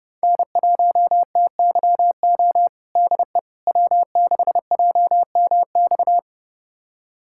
Morse code, heard as a keyed tone, about 32 wpm, 700 Hz. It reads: N1TYO DE W6JMX
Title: USA Callsign DE Callsign - 5 Characters - 30wpm